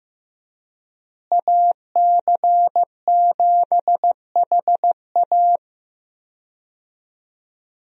Morse code, heard as a keyed tone, about 15 wpm, 700 Hz. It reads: AC7HA
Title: USA Call Signs - 5 Characters - Repeated 2x - 15wpm